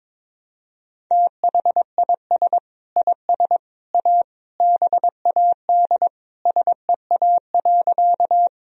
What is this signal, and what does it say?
Morse code, heard as a keyed tone, about 22 wpm, 700 Hz. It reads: THIS IS A BAD SEA.
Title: Sentences from Top 1000 Words - Review Part 1 - 22wpm